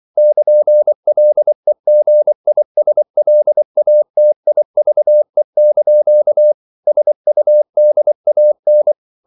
Morse code, heard as a keyed tone, about 24 wpm, 600 Hz. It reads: <KN>LEGISLATIVE) SUDAN
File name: CW / Practice / NationalCapitals24